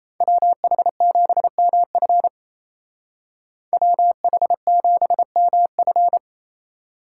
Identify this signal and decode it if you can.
Morse code, heard as a keyed tone, about 36 wpm, 700 Hz. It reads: WH7MF WH7MF